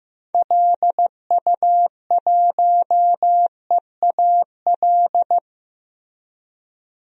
Morse code, heard as a keyed tone, about 15 wpm, 700 Hz. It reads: LU1EAL
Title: DXCC-100 Call Signs - 6 Characters - 15wpm